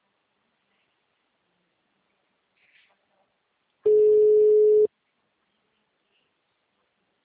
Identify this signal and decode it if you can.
Ringback tone (Europe, 425 Hz).
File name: out- -2026- - - .268